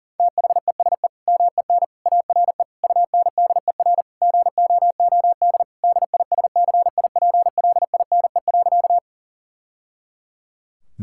Morse code, heard as a keyed tone, about 40 wpm, 700 Hz. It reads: THESE MEN ARE UNDER GOOD DISCIPLINE.
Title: Sentences from Top 1000 Words - Review Part 6 - 40wpm